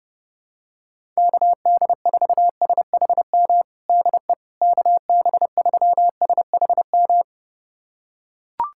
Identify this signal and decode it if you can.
Morse code, heard as a keyed tone, about 30 wpm, 700 Hz. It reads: KD4SHM DE KB3SHM
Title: USA Callsign DE Callsign - 6 Characters - 30wpm